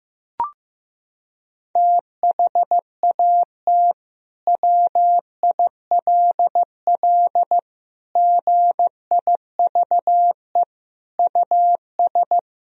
Morse code, heard as a keyed tone, about 15 wpm, 700 Hz. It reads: THAT WILL GIVE US
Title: Sentences from Top 1000 Words - Review Part 1 - 15wpm